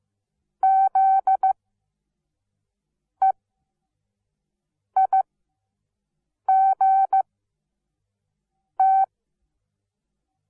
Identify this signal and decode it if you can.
Morse code, heard as a keyed tone, about 15 wpm, spaced as for 5 wpm, 770 Hz. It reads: ZEIGT
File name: Onlinelehrgang / morsen / DJ4UF Morsen17